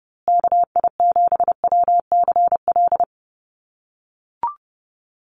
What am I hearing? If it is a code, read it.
Morse code, heard as a keyed tone, about 30 wpm, 700 Hz. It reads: KI7WCL
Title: USA Call Signs Repeated 3x 30wpm